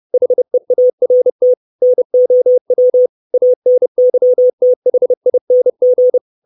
Morse code, heard as a keyed tone, about 30 wpm, 500 Hz. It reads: HEART NOW ANYTHING
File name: 632 Words 30WPM